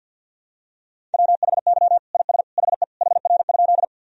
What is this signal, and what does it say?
Morse code, heard as a keyed tone, about 48 wpm, 700 Hz. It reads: WHY IS HE HR?
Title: Sentences from Top 200 Words and Abbreviations 50wpm